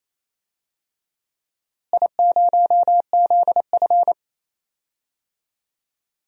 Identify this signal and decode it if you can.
Morse code, heard as a keyed tone, about 28 wpm, 700 Hz. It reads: I0ZF
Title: DXCC-100 Call Signs - 4 Characters - Repeated 2x - 28wpm